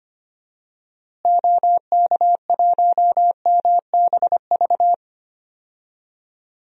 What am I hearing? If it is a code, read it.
Morse code, heard as a keyed tone, about 25 wpm, 700 Hz. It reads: OK1MBV
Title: DXCC-100 Call Signs - 6 Characters - Repeated 3x - 25wpm